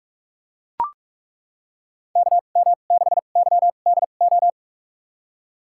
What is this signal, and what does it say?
Morse code, heard as a keyed tone, about 42 wpm, spaced as for 32 wpm, 700 Hz. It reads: KM6YDO
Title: USA Call Signs - 6 Characters - 2x Character Spacing - 45wpm